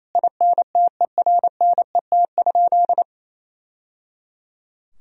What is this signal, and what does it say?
Morse code, heard as a keyed tone, about 28 wpm, 700 Hz. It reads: INTERNET?
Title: Sentences from Top 1000 Words - Review Part 4 - 28wpm